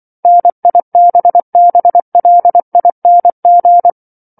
Morse code, heard as a keyed tone, about 24 wpm, 700 Hz. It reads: NIBBLING
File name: 07-Chapter-07 24 WPM